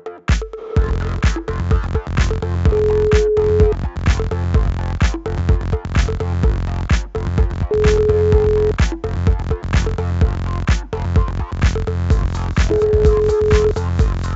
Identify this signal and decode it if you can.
Ringback tone (Europe, 425 Hz), with noise.